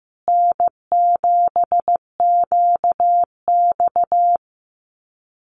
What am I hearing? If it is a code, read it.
Morse code, heard as a keyed tone, about 15 wpm, 700 Hz. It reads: N7QX
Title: USA Call Signs Repeated 3x 15wpm